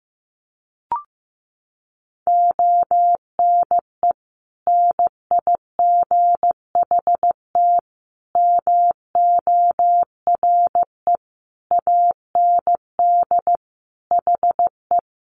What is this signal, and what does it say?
Morse code, heard as a keyed tone, about 15 wpm, 700 Hz. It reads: ONE NIGHT MORE AND HE
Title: Sentences from Top 300 Words 15wpm